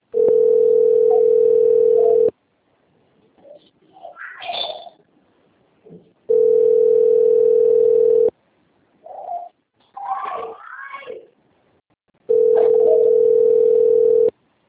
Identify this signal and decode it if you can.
Ringback tone (North America), 440 + 480 Hz.